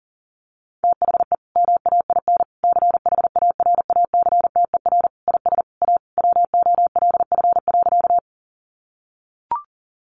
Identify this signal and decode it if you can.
Morse code, heard as a keyed tone, about 40 wpm, 700 Hz. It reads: THE MAIN CHARACTER IS A WOLF.
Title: Additional Easy Phrases 40wpm